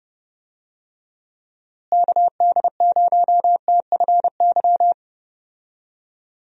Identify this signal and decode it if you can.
Morse code, heard as a keyed tone, about 30 wpm, 700 Hz. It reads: KD0TFY
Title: USA Call Signs - 6 Characters - Repeated 2x - 30wpm